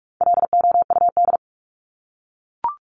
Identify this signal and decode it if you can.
Morse code, heard as a keyed tone, about 44 wpm, 700 Hz. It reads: LOUD
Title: Top 1751-2000 Words 45wpm